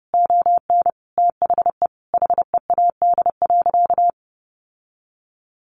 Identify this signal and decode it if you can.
Morse code, heard as a keyed tone, about 30 wpm, 700 Hz. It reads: ON THE HEAD.
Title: Sentences from Top 1200 Words 30wpm